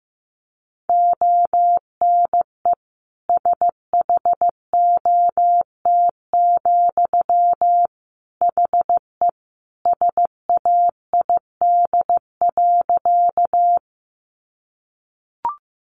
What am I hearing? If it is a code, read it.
Morse code, heard as a keyed tone, about 15 wpm, 700 Hz. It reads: ONE SHOT, HE SAID.
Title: Sentences from Top 1100 Words 15wpm